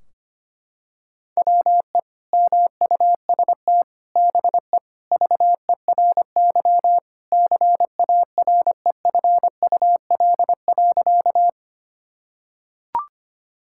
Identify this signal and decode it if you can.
Morse code, heard as a keyed tone, about 25 wpm, 700 Hz. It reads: WE MUST BE VERY CAREFUL.
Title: Sentences from Top 1600 Words 25wpm